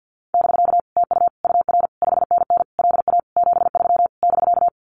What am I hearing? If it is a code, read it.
Morse code, heard as a keyed tone, about 50 wpm, 700 Hz. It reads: <BK> TU UR 5NN LA 73 <BK>
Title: Common QSO Phrases Encore 50wpm